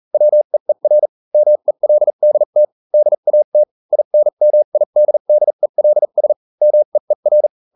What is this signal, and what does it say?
Morse code, heard as a keyed tone, about 40 wpm, spaced as for 35 wpm, 600 Hz. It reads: WEER MELDT DAT INMIDDELS MEER